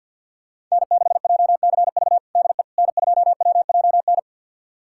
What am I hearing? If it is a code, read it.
Morse code, heard as a keyed tone, about 52 wpm, 700 Hz. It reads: N6JCV DE N2WJN
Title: USA Callsign DE Callsign - 5 Characters - 50wpm